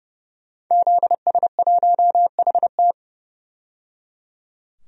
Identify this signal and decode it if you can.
Morse code, heard as a keyed tone, about 30 wpm, 700 Hz. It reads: ZS1HT
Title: DXCC-100 Call Signs - 5 Characters - 30wpm